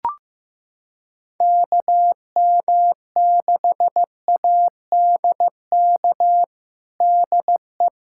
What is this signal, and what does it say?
Morse code, heard as a keyed tone, about 16 wpm, 700 Hz. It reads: KM6ADK DE
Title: USA Callsign DE Callsign - 6 Characters - 15wpm